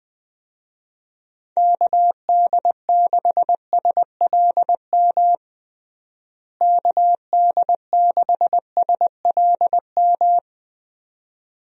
Morse code, heard as a keyed tone, about 20 wpm, 700 Hz. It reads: KD6SLM KD6SLM
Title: USA Call Signs - 6 Characters - Repeated 2x - 20wpm